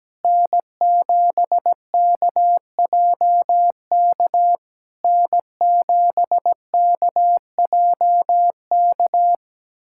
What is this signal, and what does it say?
Morse code, heard as a keyed tone, about 17 wpm, 700 Hz. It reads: N7KJK N7KJK
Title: SOTA QSO Practice 17wpm